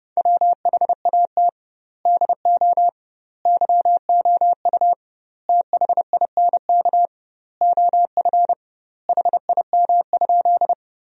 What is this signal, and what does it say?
Morse code, heard as a keyed tone, about 30 wpm, 700 Hz. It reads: WHAT DO YOU THINK OF HIM?
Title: 2x Word-Spacing - Sentences from Top 200 Words - 30wpm